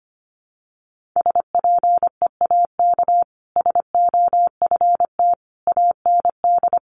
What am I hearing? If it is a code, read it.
Morse code, heard as a keyed tone, about 25 wpm, 700 Hz. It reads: SPEAK SOFT AND
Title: Sentences from Top 1500 Words 25wpm